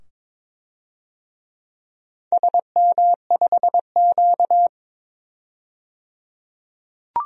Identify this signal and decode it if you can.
Morse code, heard as a keyed tone, about 22 wpm, 700 Hz. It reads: SM5Q
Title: DXCC-100 Call Signs - 4 Characters - 22wpm